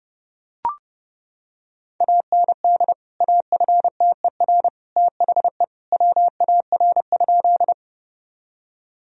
Morse code, heard as a keyed tone, about 30 wpm, 700 Hz. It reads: AND AFTER THE WAR?
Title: Sentences from Top 1000 Words - Review Part 2 - 30wpm